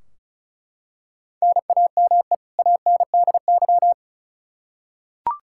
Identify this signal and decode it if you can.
Morse code, heard as a keyed tone, about 34 wpm, 700 Hz. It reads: NAME ANDY
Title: Practice QSO - Exchange 1 - Beginning - 35wpm